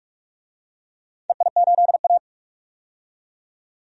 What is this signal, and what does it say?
Morse code, heard as a keyed tone, about 46 wpm, 690 Hz. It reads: EI8A
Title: DXCC-100 Call Signs - 4 Characters - 45wpm